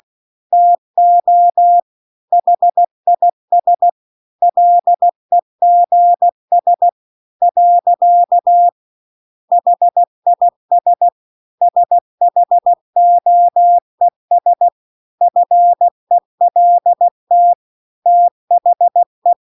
Morse code, heard as a keyed tone, about 16 wpm, 700 Hz. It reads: TO HIS LEGS . HIS SHOES FELT THE